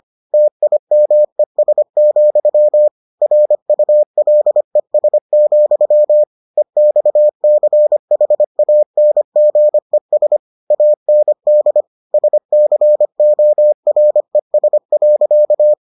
Morse code, heard as a keyed tone, about 25 wpm, 600 Hz. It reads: TIMES, RULES, EXCHANGES AND SCORES.